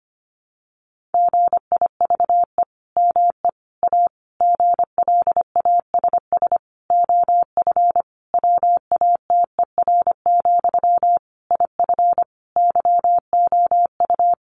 Morse code, heard as a keyed tone, about 25 wpm, 700 Hz. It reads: GIVE ME A GLASS OF WATER, IF YOU
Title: Sentences from Top 1500 Words 25wpm